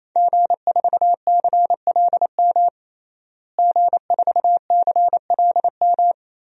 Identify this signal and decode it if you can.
Morse code, heard as a keyed tone, about 28 wpm, 700 Hz. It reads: G4CLM G4CLM
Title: DXCC-100 Call Signs - 5 Characters - Repeated 3x - 28wpm